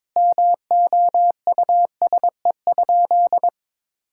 Morse code, heard as a keyed tone, about 22 wpm, 700 Hz. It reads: MOUSE?
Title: Sentences from Top 1000 Words - Review Part 5 - 22wpm